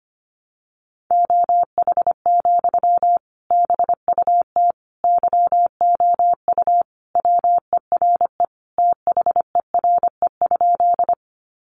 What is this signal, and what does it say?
Morse code, heard as a keyed tone, about 25 wpm, 700 Hz. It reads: OH, BUT YOU WERE THERE?
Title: Sentences from Top 500 Words 25wpm